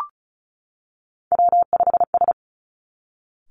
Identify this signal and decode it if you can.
Morse code, heard as a keyed tone, about 36 wpm, 700 Hz. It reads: W5S
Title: Three Challenging Letter-Number Combinations 35wpm